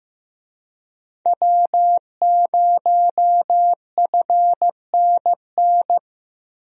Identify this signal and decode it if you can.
Morse code, heard as a keyed tone, about 16 wpm, 700 Hz. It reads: W0FNN
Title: USA Call Signs - 5 Characters - 15wpm